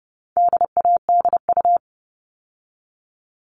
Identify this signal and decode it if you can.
Morse code, heard as a keyed tone, about 30 wpm, 700 Hz. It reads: DADU
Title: All Cut Numbers - 4-Digit 30wpm